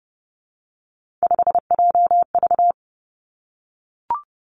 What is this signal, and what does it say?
Morse code, heard as a keyed tone, about 30 wpm, 700 Hz. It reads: 5JV
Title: Three Challenging Letter-Number Combinations 30wpm